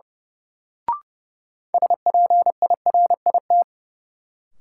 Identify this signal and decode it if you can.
Morse code, heard as a keyed tone, about 32 wpm, 700 Hz. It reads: SPIRIT
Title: Sentences from Top 1400 Words - New Word Drill - 30wpm